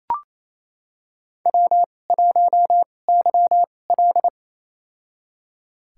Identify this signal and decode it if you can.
Morse code, heard as a keyed tone, about 28 wpm, spaced as for 20 wpm, 700 Hz. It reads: W1YL
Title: USA Call Signs - 4 Characters - 2x Character Spacing - 28wpm